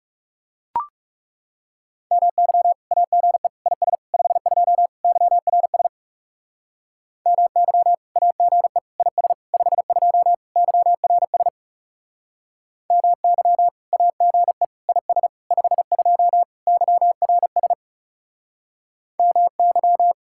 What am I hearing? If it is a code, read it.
Morse code, heard as a keyed tone, about 38 wpm, 700 Hz. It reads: MY AGE IS 52 YRS MY AGE IS 52 YRS MY AGE IS 52 YRS MY